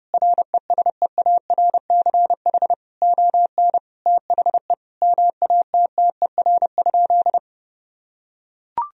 Morse code, heard as a keyed tone, about 30 wpm, 700 Hz. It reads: RESEARCH ON THE MATTER?
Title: Sentences from Top 1000 Words - Review Part 4 - 30wpm